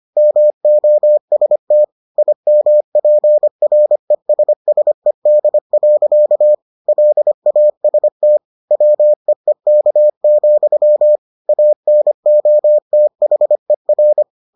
Morse code, heard as a keyed tone, about 25 wpm, 600 Hz. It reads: MOST IMPRESSED. LAST WEEK, ANOTHER